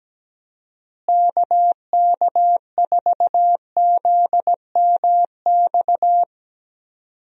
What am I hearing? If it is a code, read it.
Morse code, heard as a keyed tone, about 17 wpm, 700 Hz. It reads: KK4ZMX
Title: USA Call Signs - 6 Characters - Repeated 2x - 17wpm